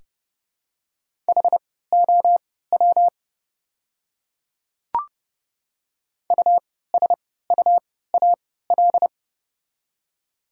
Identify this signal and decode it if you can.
Morse code, heard as a keyed tone, about 30 wpm, spaced as for 17 wpm, 700 Hz. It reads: HOW USUAL
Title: 3x Character Spacing - Top 500 Words - 30wpm